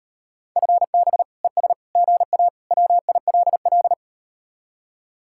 Morse code, heard as a keyed tone, about 38 wpm, 700 Hz. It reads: FB ES GA WILL